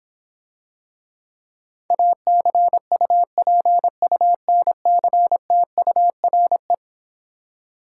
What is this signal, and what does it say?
Morse code, heard as a keyed tone, about 26 wpm, 700 Hz. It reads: ACUPUNCTURE